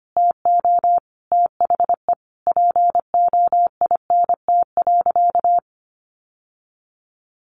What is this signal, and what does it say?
Morse code, heard as a keyed tone, about 25 wpm, 700 Hz. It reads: TO THE POINT.
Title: Sentences from Top 1400 Words 25wpm